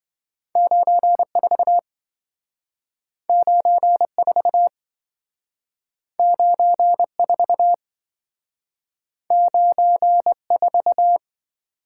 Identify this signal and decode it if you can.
Morse code, heard as a keyed tone, about 26 wpm, 700 Hz. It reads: 94 94 94 94